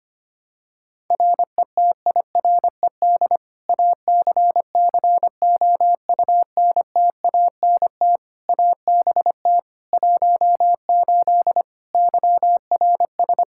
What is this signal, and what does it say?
Morse code, heard as a keyed tone, about 25 wpm, 700 Hz. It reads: RETIRED ACCOUNTANT ABT 18 YRS
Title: K1USN SST QSO Practice 25wpm